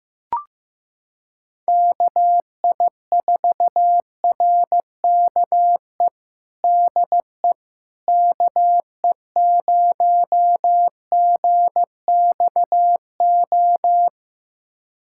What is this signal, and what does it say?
Morse code, heard as a keyed tone, about 16 wpm, 700 Hz. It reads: KI4RKE DE KE0GXO
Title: USA Callsign DE Callsign - 6 Characters - 15wpm